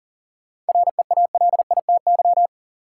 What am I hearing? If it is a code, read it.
Morse code, heard as a keyed tone, about 37 wpm, 700 Hz. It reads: REALITY
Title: Sentences from Top 1200 Words - New Word Drill Alt - 40wpm